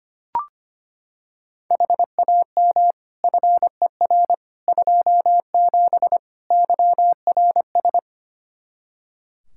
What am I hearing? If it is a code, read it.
Morse code, heard as a keyed tone, about 25 wpm, 700 Hz. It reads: HAM FER 27 YRS
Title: Practice QSO - Exchange 3 - More Info - 25wpm